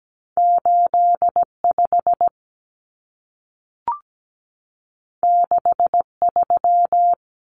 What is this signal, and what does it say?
Morse code, heard as a keyed tone, about 17 wpm, 700 Hz. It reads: 85 63
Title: Numbers - 2-Digit 17wpm